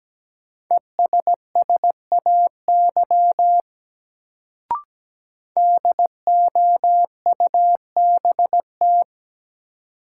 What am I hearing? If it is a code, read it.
Morse code, heard as a keyed tone, about 17 wpm, 700 Hz. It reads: ESSAY DOUBT
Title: Sentences from Top 1500 Words - New Word Drill Alt - 17wpm